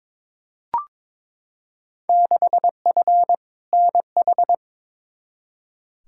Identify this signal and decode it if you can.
Morse code, heard as a keyed tone, about 22 wpm, 700 Hz. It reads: 6F NH
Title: Field Day Exchanges Encore - 22wpm